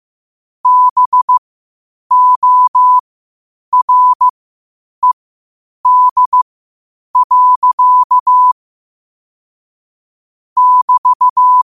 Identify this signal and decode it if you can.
Morse code, heard as a keyed tone, about 15 wpm, spaced as for 8 wpm, 1000 Hz. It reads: BORED. <BT>
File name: qotdfiles / 8wpm